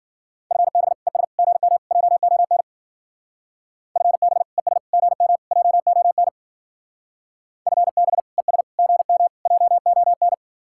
Morse code, heard as a keyed tone, about 49 wpm, 700 Hz. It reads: FB ES GM JON FB ES GM JON FB ES GM JON
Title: Speed-Racing Common QSO Phrases - 40wpm